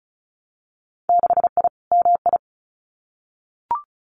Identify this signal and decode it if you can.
Morse code, heard as a keyed tone, about 36 wpm, 700 Hz. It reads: 6I MI
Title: Winter Field Day Exchanges Encore 35wpm